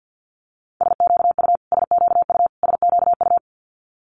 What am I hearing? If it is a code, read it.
Morse code, heard as a keyed tone, about 50 wpm, 700 Hz. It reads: SQU SQU SQU
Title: Multi-letter Phonemes Repeated 3x 50wpm